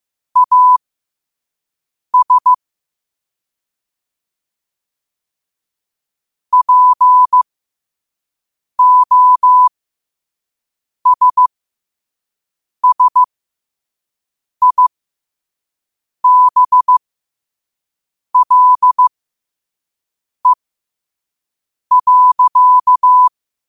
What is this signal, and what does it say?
Morse code, heard as a keyed tone, about 15 wpm, spaced as for 5 wpm, 1000 Hz. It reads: AS POSSIBLE.
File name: qotdfiles / 5wpm